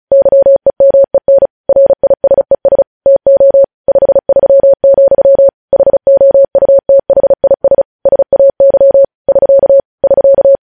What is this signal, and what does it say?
Morse code, heard as a keyed tone, about 36 wpm, 550 Hz. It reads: YEMEN RISES TO 53, HOUTHIS SAY <SK> <SK>
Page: News Headlines in Morse Code at 35 WPM